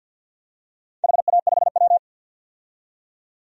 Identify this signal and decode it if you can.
Morse code, heard as a keyed tone, about 53 wpm, 700 Hz. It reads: HA5W
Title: DXCC-100 Call Signs - 4 Characters - 50wpm